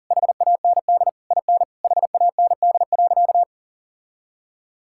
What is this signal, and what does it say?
Morse code, heard as a keyed tone, about 40 wpm, 700 Hz. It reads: HAND IN HAND.
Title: Sentences from Top 1000 Words - Review Part 5 - 40wpm